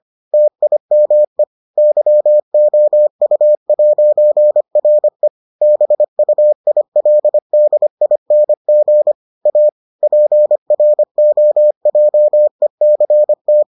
Morse code, heard as a keyed tone, about 25 wpm, 600 Hz. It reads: TIME YOU'RE BUILDING A PROJECT